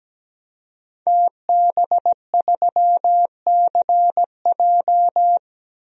Morse code, heard as a keyed tone, about 17 wpm, 700 Hz. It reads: TB3CJ
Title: DXCC-100 Call Signs - 5 Characters - Repeated 2x - 17wpm